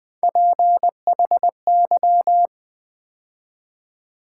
Morse code, heard as a keyed tone, about 20 wpm, 700 Hz. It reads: PHY
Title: Build A Word - 3 Letters At A Time - 8-Letter Words - 20wpm